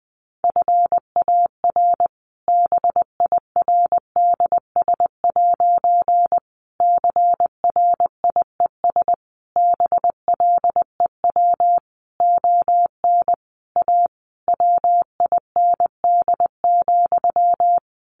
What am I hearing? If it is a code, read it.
Morse code, heard as a keyed tone, about 20 wpm, 700 Hz. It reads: FAR BIRDS' CRIES BLEW ON A WIND,
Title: A Sound Of Thunder 20wpm